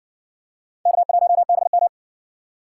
Morse code, heard as a keyed tone, about 55 wpm, 700 Hz. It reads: K1BM